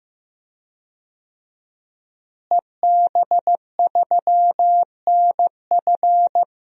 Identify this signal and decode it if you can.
Morse code, heard as a keyed tone, about 15 wpm, 700 Hz. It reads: EB3NF
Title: DXCC-100 Call Signs - 5 Characters - Repeated 3x - 15wpm